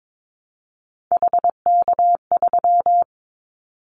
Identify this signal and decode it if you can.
Morse code, heard as a keyed tone, about 22 wpm, 700 Hz. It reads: HK3
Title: Callsign Prefixes for Top 100 DXCC - 22wpm